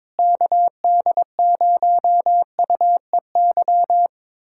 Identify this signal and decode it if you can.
Morse code, heard as a keyed tone, about 22 wpm, 700 Hz. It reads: KD0UEY
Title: USA Callsign DE Callsign 22wpm